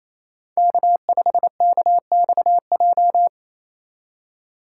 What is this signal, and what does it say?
Morse code, heard as a keyed tone, about 28 wpm, 700 Hz. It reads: K5KXJ
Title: K1USN SST QSO Practice - 28wpm